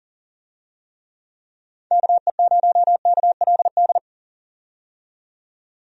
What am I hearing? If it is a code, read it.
Morse code, heard as a keyed tone, about 39 wpm, 700 Hz. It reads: KE0KLD
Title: USA Call Signs - 6 Characters - 40wpm